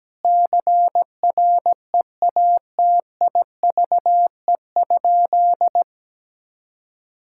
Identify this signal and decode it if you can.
Morse code, heard as a keyed tone, about 17 wpm, 700 Hz. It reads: CREATIVE?
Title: Sentences from Top 1000 Words - Review Part 5 - 17wpm